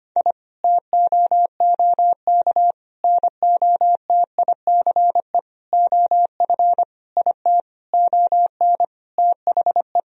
Morse code, heard as a keyed tone, about 25 wpm, 700 Hz. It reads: I TOOK NOTICE OF IT ON THE
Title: Sentences from Top 1000 Words - Review Part 2 - 25wpm